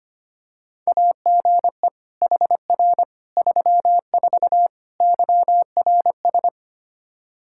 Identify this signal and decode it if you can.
Morse code, heard as a keyed tone, about 25 wpm, 700 Hz. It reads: AGE HR 34 YRS
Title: Practice QSO - Exchange 3 - More Info - 25wpm